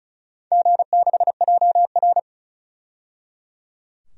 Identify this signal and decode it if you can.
Morse code, heard as a keyed tone, about 36 wpm, 700 Hz. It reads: G6JR
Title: DXCC-100 Call Signs - 4 Characters - 35wpm